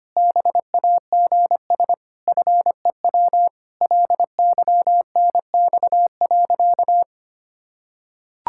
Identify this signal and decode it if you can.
Morse code, heard as a keyed tone, about 25 wpm, 700 Hz. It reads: BAGS FEW LYNX.
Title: Pangram Challenge v2 - 25wpm